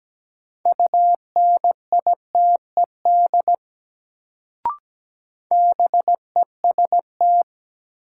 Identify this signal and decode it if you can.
Morse code, heard as a keyed tone, about 17 wpm, 700 Hz. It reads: UNITED BEST
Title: Sets of 2 Words - New Word Drill - 17wpm